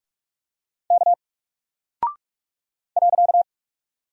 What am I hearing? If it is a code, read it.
Morse code, heard as a keyed tone, about 44 wpm, 700 Hz. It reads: K .
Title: Single Character (Letter:Number:Punctuation:Prosign) Alt 45wpm